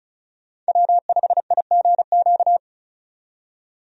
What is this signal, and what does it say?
Morse code, heard as a keyed tone, about 34 wpm, 700 Hz. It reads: W5IGQ
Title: USA Call Signs - 5 Characters - 35wpm